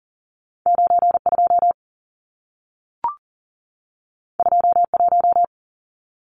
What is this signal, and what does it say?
Morse code, heard as a keyed tone, about 40 wpm, 700 Hz. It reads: 92 21
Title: Numbers - 2-Digit 40wpm